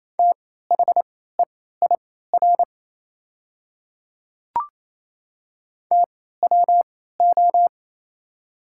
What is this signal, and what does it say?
Morse code, heard as a keyed tone, about 28 wpm, spaced as for 16 wpm, 700 Hz. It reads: THEIR TWO
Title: 3x Character Spacing - Top 100 Words - 28wpm